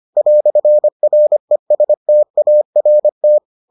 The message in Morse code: "RESTART

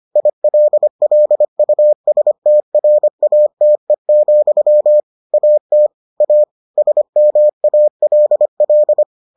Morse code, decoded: ILLUSTRATE, AT A SMALL

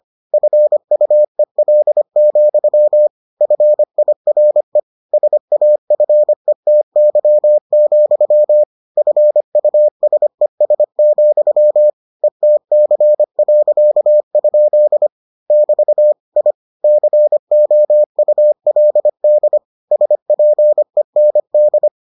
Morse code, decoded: FUEL, FIRE SAFETY, FUSES, ETC.? <BT> I COULD SPEND

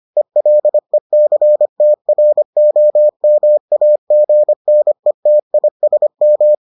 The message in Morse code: ELECTROMAGNETISM

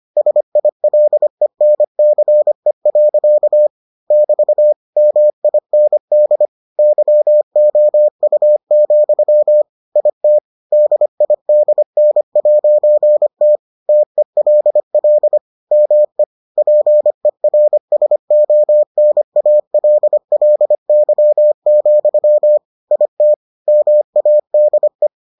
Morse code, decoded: SILENCE. <BT> MIND YOU, IT DIDN'T TELL ME PERSONALLY, IT MADE

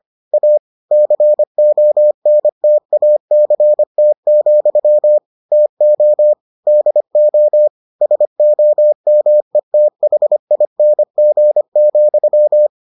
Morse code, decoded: A CONTACT, TO DO SOMETHING,

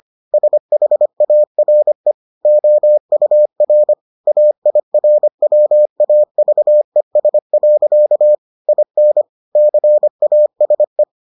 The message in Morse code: SHARE OUR AIRWAVES. IN CASE

